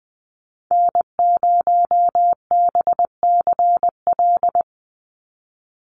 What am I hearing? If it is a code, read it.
Morse code, heard as a keyed tone, about 20 wpm, 700 Hz. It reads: N0BCL
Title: USA Call Signs 20wpm